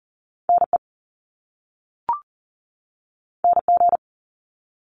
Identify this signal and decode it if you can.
Morse code, heard as a keyed tone, about 38 wpm, 700 Hz. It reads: NE NG
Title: Top Two Letter Combinations in Words 40wpm